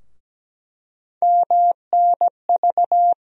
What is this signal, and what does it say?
Morse code, heard as a keyed tone, about 17 wpm, 700 Hz. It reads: MNV
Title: USA Call Sign Postfixes - 3 Letters 17wpm